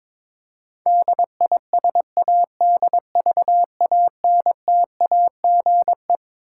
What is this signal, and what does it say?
Morse code, heard as a keyed tone, about 22 wpm, 700 Hz. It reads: DISADVANTAGE
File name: Twelve Letter Words - 22wpm